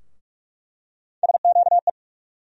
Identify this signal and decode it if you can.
Morse code, heard as a keyed tone, about 46 wpm, 700 Hz. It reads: SQE
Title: USA Call Sign Postfixes - 3 Letters 45wpm